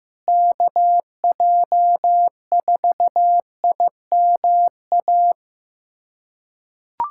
Morse code, heard as a keyed tone, about 14 wpm, 700 Hz. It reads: KJ4IMA
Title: USA Callsign DE Callsign - 6 Characters - 15wpm